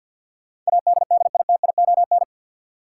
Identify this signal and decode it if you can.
Morse code, heard as a keyed tone, about 51 wpm, 700 Hz. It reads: ADDITION
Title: Sentences from Top 1500 Words - New Word Drill Alt - 50wpm